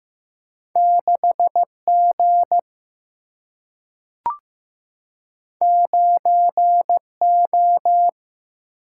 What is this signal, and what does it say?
Morse code, heard as a keyed tone, about 15 wpm, 700 Hz. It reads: 6 G 9 O
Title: Single-Letter Single-Number 15wpm